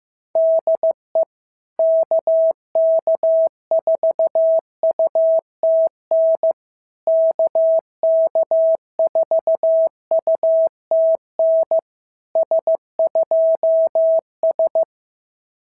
Morse code, heard as a keyed tone, about 15 wpm, 650 Hz. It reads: DE KK4UTN KK4UTN S2S
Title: SOTA QSO Practice 15wpm